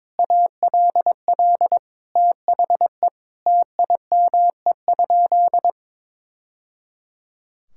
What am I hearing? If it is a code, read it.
Morse code, heard as a keyed tone, about 21 wpm, 700 Hz. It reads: ALL THE TIME?
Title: Sentences from Top 1000 Words - Review Part 5 - 22wpm